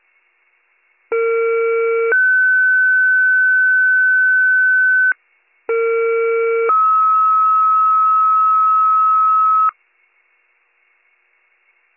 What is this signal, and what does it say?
Two-tone page, heard as a one-second tone, then a three-second tone, about 460 then 1500 Hz, about 460 then 1290 Hz. It is heard from 1.1 to 9.7 s.